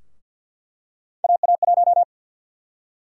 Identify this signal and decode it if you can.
Morse code, heard as a keyed tone, about 47 wpm, 700 Hz. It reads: AA1
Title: USA Call Sign Prefixes - Repeated 2x - 50wpm